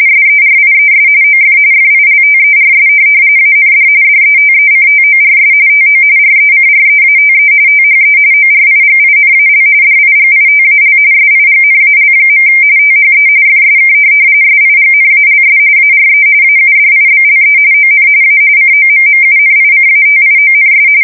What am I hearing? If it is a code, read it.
RTTY, about 45.45 baud. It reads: SERIES OF FILTERS AND DETECTORS TO RECREATE THE ORIGINAL DIGITAL SIGNAL. THE FSK SIGNALS ARE AUDIBLE ON A COMMUNICATIONS
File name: fsk170 / teletype-RTTY